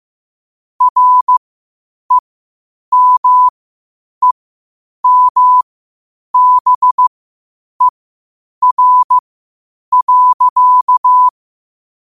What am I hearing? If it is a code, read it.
Morse code, heard as a keyed tone, about 15 wpm, spaced as for 8 wpm, 1000 Hz. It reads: REMEMBER.